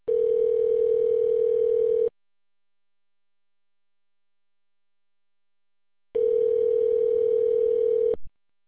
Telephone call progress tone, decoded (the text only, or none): ringback tone